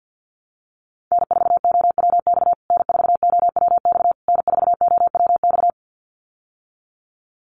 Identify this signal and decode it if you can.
Morse code, heard as a keyed tone, about 50 wpm, 700 Hz. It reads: N4OWX N4OWX N4OWX
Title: USA Call Signs Repeated 3x 50wpm